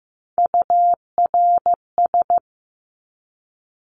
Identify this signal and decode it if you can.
Morse code, heard as a keyed tone, about 15 wpm, 700 Hz. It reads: URS
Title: Abbreviations 15wpm